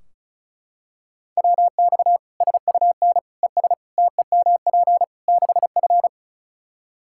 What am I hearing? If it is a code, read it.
Morse code, heard as a keyed tone, about 36 wpm, 700 Hz. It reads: WX SUN ES TEMP 6F
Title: Practice QSO - Exchange 2 - Info - 35wpm